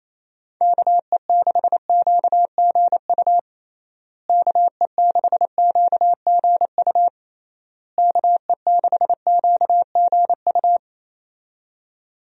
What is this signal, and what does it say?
Morse code, heard as a keyed tone, about 28 wpm, 700 Hz. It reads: KE6QGU KE6QGU KE6QGU
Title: USA Call Signs - 6 Characters - Repeated 3x - 28wpm